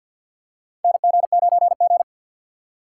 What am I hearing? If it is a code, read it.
Morse code, heard as a keyed tone, about 52 wpm, 700 Hz. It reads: NG9G
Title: USA Call Signs - 4 Characters - 3x Character Spacing - 50wpm